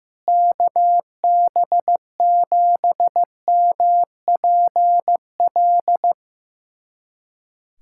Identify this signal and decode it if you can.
Morse code, heard as a keyed tone, about 15 wpm, 700 Hz. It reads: KB7MPL
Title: USA Callsign DE Callsign - 6 Characters - 15wpm